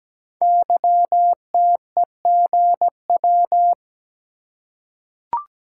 Morse code, heard as a keyed tone, about 17 wpm, 700 Hz. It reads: YTEGW
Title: Code Groups - 5 Letter:Numbers - 17wpm